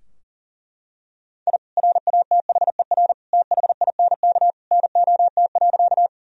Morse code, heard as a keyed tone, about 40 wpm, 700 Hz. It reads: I RATHER THINK NOT.
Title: Sentences from Top 1000 Words - Review Part 4 - 40wpm